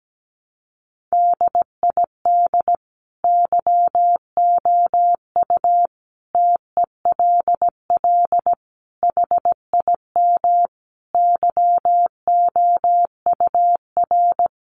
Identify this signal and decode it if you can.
Morse code, heard as a keyed tone, about 17 wpm, 700 Hz. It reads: DID YOU TELL HIM YOUR
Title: Sentences from Top 400 Words 17wpm